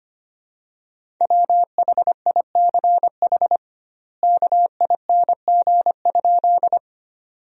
Morse code, heard as a keyed tone, about 25 wpm, 700 Hz. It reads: WHICH KING?
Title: 2x Word-Spacing - Sentences from Top 400 Words - 25wpm